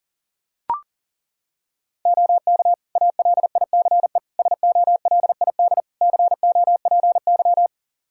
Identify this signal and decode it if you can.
Morse code, heard as a keyed tone, about 40 wpm, 700 Hz. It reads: OK ALICE SOLID COPY
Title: Practice QSO - Exchange 3 - More Info - 40wpm